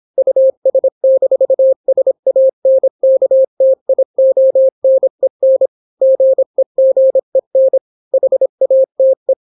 Morse code, heard as a keyed tone, about 26 wpm, 530 Hz. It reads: US-SANKTIONEN GEGEN HATE